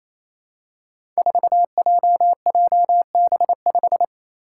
Morse code, heard as a keyed tone, about 29 wpm, 700 Hz. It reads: 4JJB5
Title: Code Groups - 5 Letter:Numbers - 28wpm